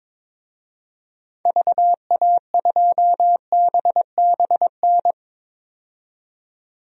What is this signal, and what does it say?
Morse code, heard as a keyed tone, about 22 wpm, 700 Hz. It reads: VA2BBN
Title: DXCC-100 Call Signs - 6 Characters - 22wpm